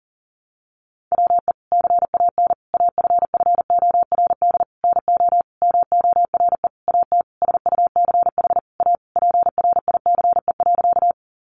Morse code, heard as a keyed tone, about 40 wpm, 700 Hz. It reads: WE CAN AFFORD NO MORE AT SUCH A PRICE.
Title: Sentences from Top 1100 Words 40wpm